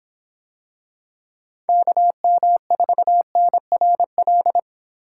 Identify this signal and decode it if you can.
Morse code, heard as a keyed tone, about 24 wpm, 700 Hz. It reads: KM4NRL